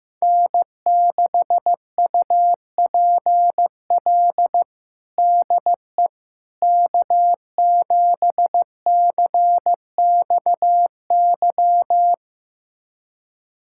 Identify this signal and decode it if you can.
Morse code, heard as a keyed tone, about 15 wpm, 700 Hz. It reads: N6UPL DE K7CXY
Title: USA Callsign DE Callsign - 5 Characters - 15wpm